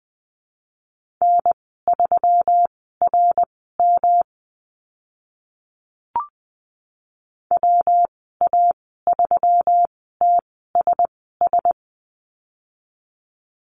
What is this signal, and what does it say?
Morse code, heard as a keyed tone, about 20 wpm, spaced as for 14 wpm, 700 Hz. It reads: N3RM WA3TSS
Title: 154 Lesson 25g - Letter p - Call Signs 20wpm